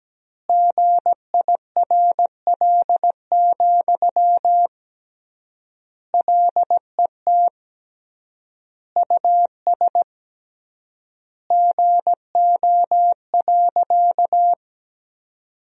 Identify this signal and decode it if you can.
Morse code, heard as a keyed tone, about 17 wpm, 700 Hz. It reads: GIRL, LET US GO.
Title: 3x Word-Spacing - Sentences from Top 400 Words - 17wpm